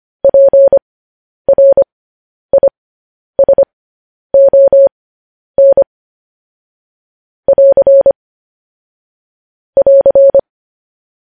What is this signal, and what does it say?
Morse code, heard as a keyed tone, about 25 wpm, spaced as for 10 wpm, 550 Hz. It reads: PRISON <AR> <AR>
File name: podcast / News headlines-1x-25x10